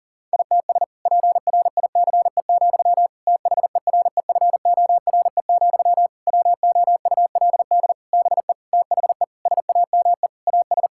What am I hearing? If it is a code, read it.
Morse code, heard as a keyed tone, about 40 wpm, 700 Hz. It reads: ITS PRICE, THEREFORE, WOULD BE THE SAME AS
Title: Sentences from Top 1000 Words - Review Part 5 - 40wpm